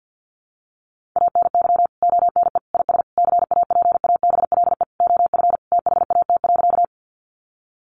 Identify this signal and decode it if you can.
Morse code, heard as a keyed tone, about 50 wpm, 700 Hz. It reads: ANY ONE IS CAPABLE OF THAT.
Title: Sentences from Top 1200 Words 50wpm